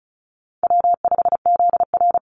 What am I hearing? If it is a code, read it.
Morse code, heard as a keyed tone, about 34 wpm, 700 Hz. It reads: W5ZR
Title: Easy Call Signs Repeated 2x - 35wpm